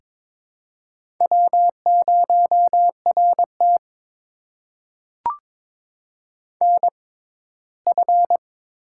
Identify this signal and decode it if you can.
Morse code, heard as a keyed tone, about 22 wpm, spaced as for 15 wpm, 700 Hz. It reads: W0RT NF
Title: USA Call Signs - 4 Characters - 6x Character Spacing - 22wpm